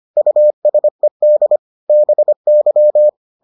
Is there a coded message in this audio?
Yes: Morse code from 0.2 to 3.1 s, heard as a keyed tone, about 25 wpm, 600 Hz. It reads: USED BY